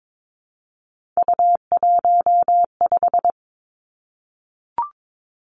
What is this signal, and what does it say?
Morse code, heard as a keyed tone, about 22 wpm, 700 Hz. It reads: U15
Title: Three Challenging Letter-Number Combinations 22wpm